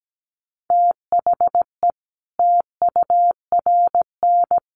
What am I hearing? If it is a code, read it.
Morse code, heard as a keyed tone, about 17 wpm, 700 Hz. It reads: THE TURN
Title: Sets of 4 Words 17wpm